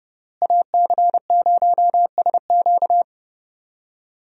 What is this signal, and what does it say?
Morse code, heard as a keyed tone, about 30 wpm, 700 Hz. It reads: AC0SQ
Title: USA Call Signs - 5 Characters - Repeated 3x - 30wpm